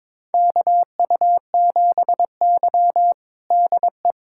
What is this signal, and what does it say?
Morse code, heard as a keyed tone, about 22 wpm, 700 Hz. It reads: KU7Y DE
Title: USA Callsign DE Callsign - 4 Characters - 22wpm